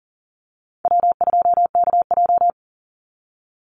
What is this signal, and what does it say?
Morse code, heard as a keyed tone, about 40 wpm, 700 Hz. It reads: W2KJ
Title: Easy Call Signs Repeated 3x - 40wpm